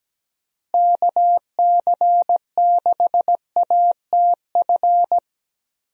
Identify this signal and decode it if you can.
Morse code, heard as a keyed tone, about 17 wpm, 700 Hz. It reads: KC6ATF
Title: USA Call Signs - 6 Characters - Repeated 3x - 17wpm